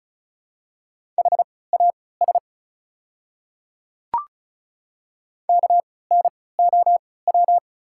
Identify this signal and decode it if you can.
Morse code, heard as a keyed tone, about 36 wpm, spaced as for 20 wpm, 700 Hz. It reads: HAS KNOW
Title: 3x Character Spacing - Top 100 Words - 35wpm